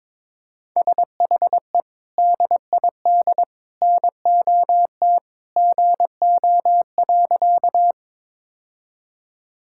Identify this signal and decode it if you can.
Morse code, heard as a keyed tone, about 22 wpm, 700 Hz. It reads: SHE DID NOT GO.
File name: practice / 22wpm / Sentences with 3-Word Phrases - 22wpm